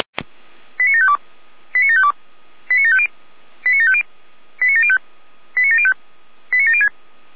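Five-tone selective call (ZVEI), heard as quick runs of five tones, tones of about 70 ms. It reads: 89752 89752 89750 89750 89705 89705 89706
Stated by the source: ZVEI five-tone selective call received with an SDR.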